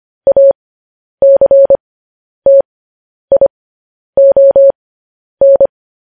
Morse code, heard as a keyed tone, about 25 wpm, spaced as for 10 wpm, 550 Hz. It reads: ACTION
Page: news-headlines-in-morse-code-at-10-wpm